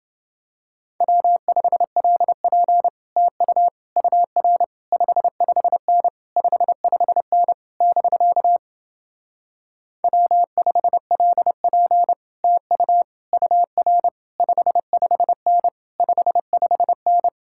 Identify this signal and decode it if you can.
Morse code, heard as a keyed tone, about 29 wpm, 700 Hz. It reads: W5LP TU UR 55N 55N <BK> W5LP TU UR 55N 55N